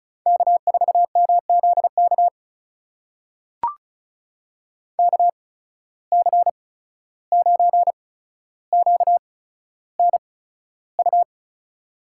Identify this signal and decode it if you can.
Morse code, heard as a keyed tone, about 36 wpm, spaced as for 14 wpm, 700 Hz. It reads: K4MZK KC9QNU
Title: USA Call Signs - 8x Character Spacing - 35wpm